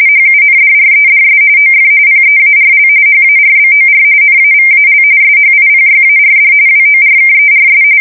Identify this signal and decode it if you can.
RTTY, about 45.45 baud. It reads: HUNDREDS OF PEARL HARBOR NAVAL SHIPYARD WORKERS
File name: fsk170 / USS-Missouri